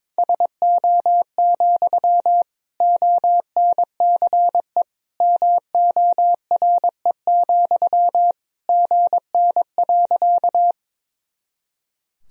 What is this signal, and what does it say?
Morse code, heard as a keyed tone, about 22 wpm, 700 Hz. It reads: SO, ONCE MORE, GN.
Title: Sentences from Top 300 Words and Abbreviations 22wpm